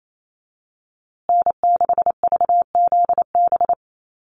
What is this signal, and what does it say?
Morse code, heard as a keyed tone, about 28 wpm, 700 Hz. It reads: N6VZB
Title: USA Call Signs Repeated 3x 28wpm